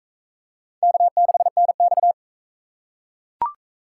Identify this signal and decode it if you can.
Morse code, heard as a keyed tone, about 40 wpm, 700 Hz. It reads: K6NX